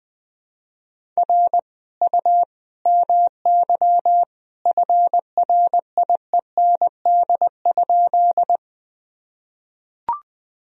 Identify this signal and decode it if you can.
Morse code, heard as a keyed tone, about 20 wpm, 700 Hz. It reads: R U MY FRIEND?
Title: Sentences from Top 300 Words and Abbreviations 20wpm